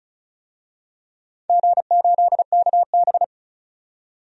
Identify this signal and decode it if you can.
Morse code, heard as a keyed tone, about 36 wpm, 700 Hz. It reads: G8KB
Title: DXCC-100 Call Signs - 4 Characters - 35wpm